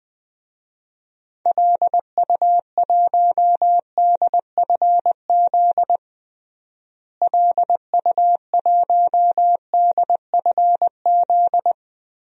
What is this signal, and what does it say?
Morse code, heard as a keyed tone, about 20 wpm, 700 Hz. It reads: LU1DFZ LU1DFZ
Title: DXCC-100 Call Signs - 6 Characters - Repeated 2x - 20wpm